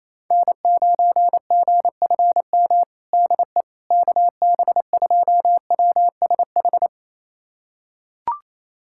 Morse code, heard as a keyed tone, about 28 wpm, 700 Hz. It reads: N9GFM DE KB2WSH
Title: USA Callsign DE Callsign 28wpm